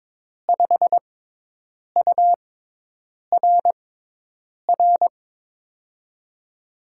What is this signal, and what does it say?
Morse code, heard as a keyed tone, about 22 wpm, spaced as for 8 wpm, 700 Hz. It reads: 5URR
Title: USA Call Signs - 5 Characters - 6x Character Spacing - 22wpm